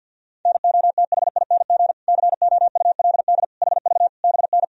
Morse code, heard as a keyed tone, about 52 wpm, 700 Hz. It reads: NOTHING COULD HV BN
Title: Sentences from Top 500 Words and Abbreviations 50wpm